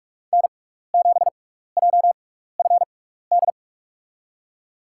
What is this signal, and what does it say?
Morse code, heard as a keyed tone, about 47 wpm, spaced as for 16 wpm, 700 Hz. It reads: N7JFD
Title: USA Call Signs - 5 Characters - 6x Character Spacing - 45wpm